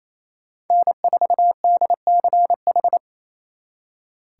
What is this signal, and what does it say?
Morse code, heard as a keyed tone, about 35 wpm, 700 Hz. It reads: N4DCH